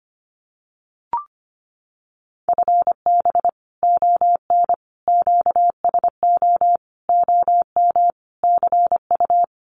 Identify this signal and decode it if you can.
Morse code, heard as a keyed tone, about 25 wpm, 700 Hz. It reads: FB ON QSO OM CU
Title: Common QSO Phrases 25wpm